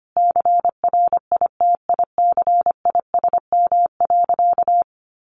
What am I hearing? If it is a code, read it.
Morse code, heard as a keyed tone, about 25 wpm, 700 Hz. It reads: CRITICISM.
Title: Sentences from Top 1500 Words 25wpm